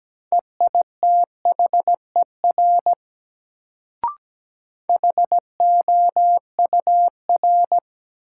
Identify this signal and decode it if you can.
Morse code, heard as a keyed tone, about 17 wpm, 700 Hz. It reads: EITHER HOUR
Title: Sets of 5 Words - New Word Drill - 17wpm